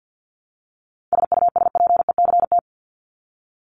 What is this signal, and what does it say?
Morse code, heard as a keyed tone, about 50 wpm, 700 Hz. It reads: SUSPECT
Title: Top 1751-2000 Words 50wpm